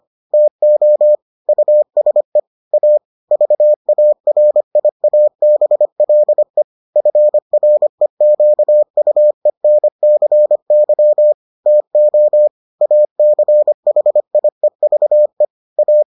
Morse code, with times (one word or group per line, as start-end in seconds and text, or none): 0.3-1.2 TO
1.5-2.4 USE
2.7-3.0 A
3.3-6.6 VARIABLE
7.0-11.3 FREQUENCY
11.7-12.5 TO
12.8-15.4 ACHIEVE
15.8-16.0 A